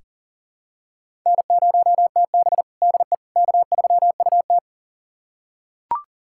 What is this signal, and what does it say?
Morse code, heard as a keyed tone, about 40 wpm, 700 Hz. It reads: N0TB DE K3UT
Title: USA Callsign DE Callsign - 4 Characters - 40wpm